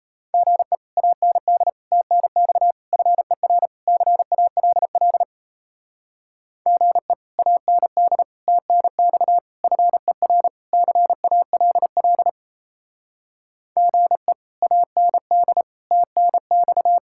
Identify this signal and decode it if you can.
Morse code, heard as a keyed tone, about 35 wpm, 700 Hz. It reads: GE AND TNX FER CALL GE AND TNX FER CALL GE AND TNX